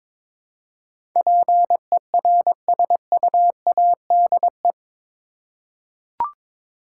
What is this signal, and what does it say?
Morse code, heard as a keyed tone, about 22 wpm, 700 Hz. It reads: PERSUADE
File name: practice / 22wpm / Eight Letter Words - 22wpm